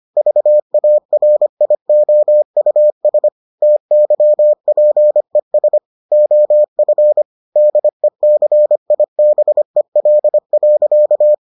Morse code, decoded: VARIOUS TYPES OF DECIBEL.

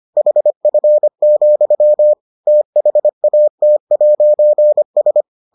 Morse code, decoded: HF, THAT'S

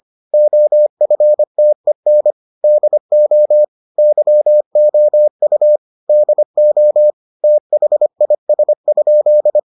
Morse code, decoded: OFTEN DO YOU DO THIS?

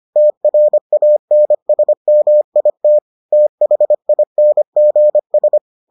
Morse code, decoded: TRANSMIT THINGS

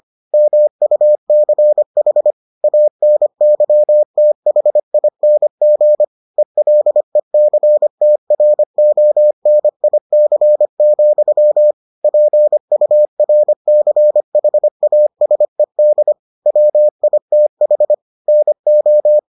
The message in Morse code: MUCH ANYTHING ELECTRONIC, PURCHASED WITH NO

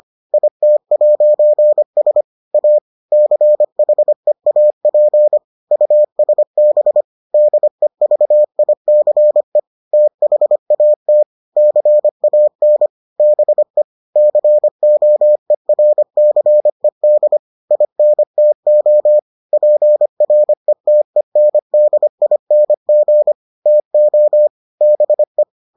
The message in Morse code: IT'S A CHEAP USB DEVICE THAT CAN BE COERCED INTO PRETENDING TO BE

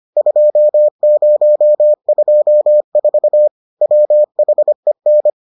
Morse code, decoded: 2024 WHEN

